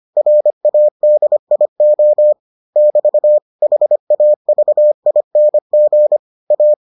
Morse code, heard as a keyed tone, about 25 wpm, 600 Hz. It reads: RADIO <BT> HAVING A